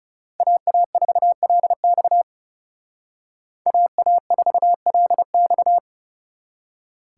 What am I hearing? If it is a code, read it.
Morse code, heard as a keyed tone, about 38 wpm, 700 Hz. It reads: AA4LX AA4LX